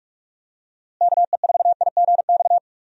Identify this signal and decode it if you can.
Morse code, heard as a keyed tone, about 44 wpm, 700 Hz. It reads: KE4IGX
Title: USA Call Signs - 6 Characters - Repeated 3x - 45wpm